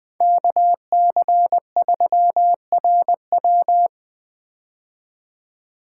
Morse code, heard as a keyed tone, about 20 wpm, 700 Hz. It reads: KC3RW
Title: 2023 POTA Activator Call Signs - 20wpm